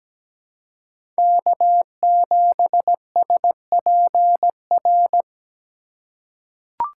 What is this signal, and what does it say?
Morse code, heard as a keyed tone, about 17 wpm, 700 Hz. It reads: K7SPR
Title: USA Call Signs - 5 Characters - 2x Character Spacing - 17wpm